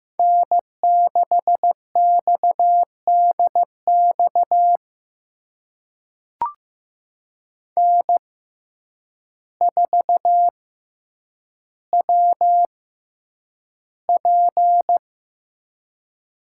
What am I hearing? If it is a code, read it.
Morse code, heard as a keyed tone, about 15 wpm, spaced as for 8 wpm, 700 Hz. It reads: N6XDX N4WP
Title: USA Call Signs - 5 Characters - 6x Character Spacing - 15wpm